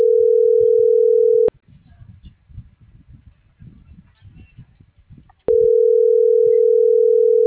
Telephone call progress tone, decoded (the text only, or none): ringback tone